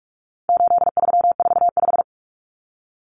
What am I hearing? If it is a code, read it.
Morse code, heard as a keyed tone, about 44 wpm, 700 Hz. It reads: 8345
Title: Common Cut Numbers - 4-Digit 45wpm